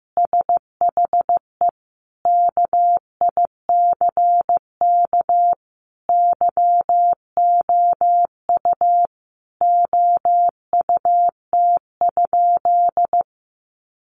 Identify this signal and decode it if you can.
Morse code, heard as a keyed tone, about 15 wpm, 700 Hz. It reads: SHE KICK YOU OUT?
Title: Sentences from Top 1300 Words 15wpm